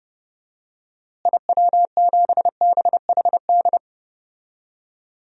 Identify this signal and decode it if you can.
Morse code, heard as a keyed tone, about 30 wpm, 700 Hz. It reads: IW7BHD
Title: DXCC-100 Call Signs - 6 Characters - Repeated 3x - 30wpm